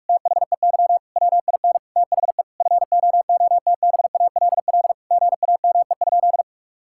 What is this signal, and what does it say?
Morse code, heard as a keyed tone, about 42 wpm, 700 Hz. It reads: THEY WIN THE FOOTBALL GAME?
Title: Sentences from Top 1600 Words 45wpm